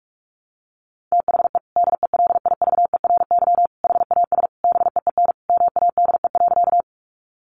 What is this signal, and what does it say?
Morse code, heard as a keyed tone, about 44 wpm, 700 Hz. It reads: THE DELIVERY HAS BEEN MADE.
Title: Sentences from Top 1500 Words 45wpm